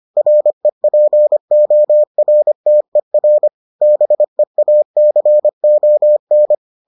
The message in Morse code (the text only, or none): REPORTER BEACON